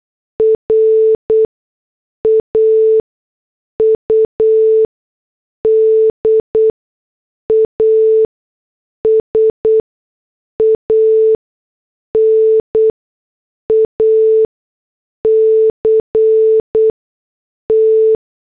Morse code, heard as a keyed tone, about 8 wpm, spaced as for 6 wpm, 440 Hz. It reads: RAUD AS AN ACT